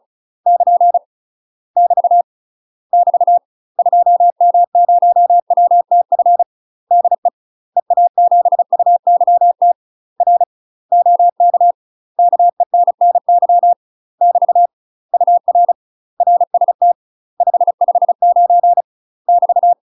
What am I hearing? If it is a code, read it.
Morse code, heard as a keyed tone, about 33 wpm, 700 Hz. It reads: <KN> <BT> <BT> 2M0WTF DE EA7UYT R OK KENNY <BT> UR RST 559 <BT>
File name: UK MASTER QSOs TRAINING/35WPM